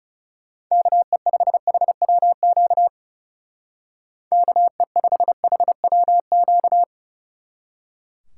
Morse code, heard as a keyed tone, about 36 wpm, 700 Hz. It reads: KE5HWQ KE5HWQ
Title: Speed Racing - USA 2x3 Call Signs 30wpm